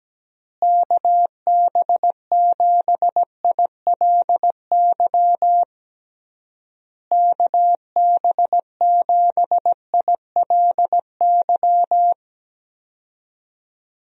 Morse code, heard as a keyed tone, about 17 wpm, 700 Hz. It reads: KB7ILY KB7ILY
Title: USA Call Signs - 6 Characters - Repeated 3x - 17wpm